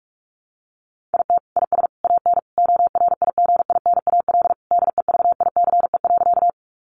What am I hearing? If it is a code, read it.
Morse code, heard as a keyed tone, about 44 wpm, 700 Hz. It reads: IT IS AN ORIGINAL DEVICE.
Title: Sentences from Top 1500 Words 45wpm